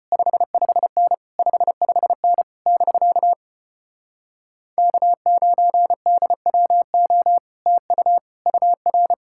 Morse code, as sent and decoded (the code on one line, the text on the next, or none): ..... ..... -. / ..... ..... -. / -...-.- / -.- ----. -.. .-- --- / - ..- / ..- .-.
55N 55N <BK> K9DWO TU UR